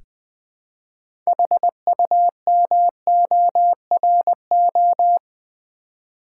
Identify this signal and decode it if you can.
Morse code, heard as a keyed tone, about 20 wpm, 700 Hz. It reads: HUMORO
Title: Build A Word - 3 Letters At A Time - 8-Letter Words - 20wpm